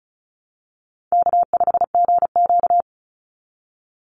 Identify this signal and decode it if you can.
Morse code, heard as a keyed tone, about 36 wpm, 700 Hz. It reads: K5GQ
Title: Call Signs - 35wpm